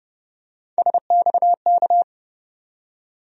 Morse code, heard as a keyed tone, about 30 wpm, 700 Hz. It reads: SXK
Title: USA Call Sign Postfixes - 3 Letters 30wpm